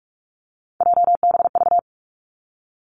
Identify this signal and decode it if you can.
Morse code, heard as a keyed tone, about 46 wpm, 700 Hz. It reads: JBV
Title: Three Challenging Letter-Number Combinations 45wpm